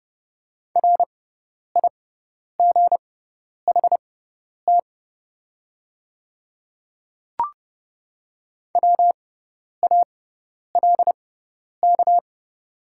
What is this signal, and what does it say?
Morse code, heard as a keyed tone, about 30 wpm, spaced as for 10 wpm, 700 Hz. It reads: RIGHT WALK
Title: 6x Character Spacing - Top 300 Words - 30wpm